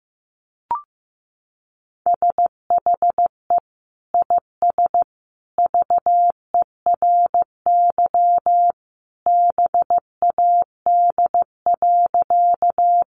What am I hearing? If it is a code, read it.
Morse code, heard as a keyed tone, about 15 wpm, 700 Hz. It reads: SHE IS VERY BAD.
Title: Sentences from Top 700 Words 15wpm